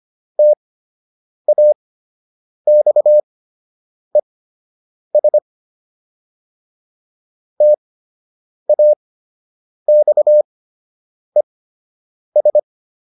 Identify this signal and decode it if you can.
Morse code, heard as a keyed tone, about 25 wpm, spaced as for 8 wpm, 600 Hz. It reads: TAXES TAXES